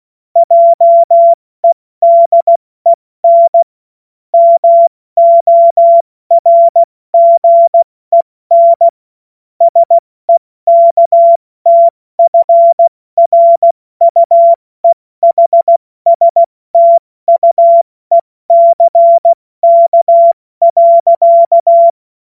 Morse code, heard as a keyed tone, about 16 wpm, 680 Hz. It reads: JEDEN MORGEN SEKTFRUEHSTUECK.
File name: Morsekurs / Witze / Witz 29. November 2022 70 F80